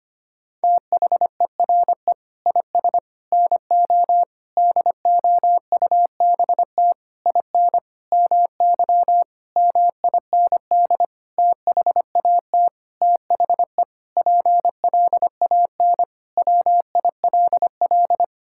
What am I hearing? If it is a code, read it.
Morse code, heard as a keyed tone, about 25 wpm, 700 Hz. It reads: THERE IS NO DOUBT IN MY MIND THAT THE PLAN WILL
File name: practice / 25wpm / Sentences with 9-Word Phrases - 25wpm